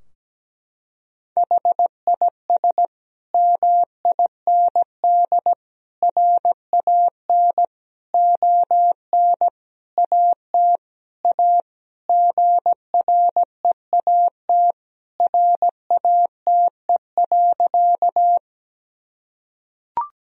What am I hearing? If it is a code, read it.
Morse code, heard as a keyed tone, about 17 wpm, 700 Hz. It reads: HIS MIND RAN ON AT A GREAT RATE.
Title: Sentences from Top 1000 Words - Review Part 4 - 17wpm